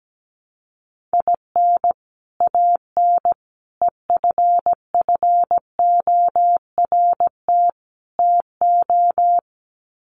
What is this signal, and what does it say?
Morse code, heard as a keyed tone, about 17 wpm, 700 Hz. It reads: IN AN EFFORT TO
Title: Sets of 4 Words 17wpm